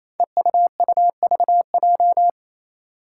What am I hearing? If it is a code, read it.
Morse code, heard as a keyed tone, about 27 wpm, 700 Hz. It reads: EUUVJ
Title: Five Challenging Letter-Number Combinations 28wpm